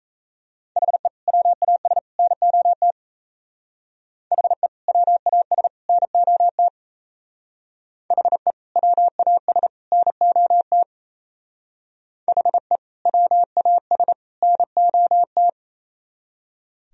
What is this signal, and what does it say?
Morse code, heard as a keyed tone, about 34 wpm, 700 Hz. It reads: HE WAS NOT HE WAS NOT HE WAS NOT HE WAS NOT
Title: Speed-Racing - Sets of 3 Words 28wpm